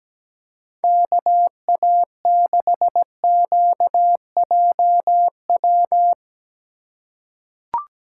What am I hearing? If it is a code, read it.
Morse code, heard as a keyed tone, about 17 wpm, 700 Hz. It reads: KA6QJW
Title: USA Call Signs - 6 Characters - 3x Character Spacing - 17wpm